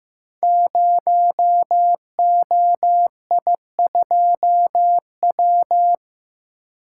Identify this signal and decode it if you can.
Morse code, heard as a keyed tone, about 15 wpm, 700 Hz. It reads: 0OI2W
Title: Code Groups - 5 Letter:Numbers - 15wpm